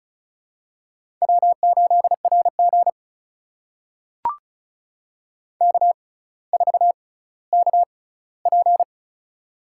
Morse code, heard as a keyed tone, about 34 wpm, spaced as for 18 wpm, 700 Hz. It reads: W8RG K4KP
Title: USA Call Signs - 4 Characters - 6x Character Spacing - 35wpm